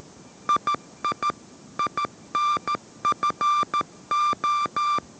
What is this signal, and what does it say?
Morse code, heard as a keyed tone, about 13 wpm, 1180 Hz. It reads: II INFO
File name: 14wpm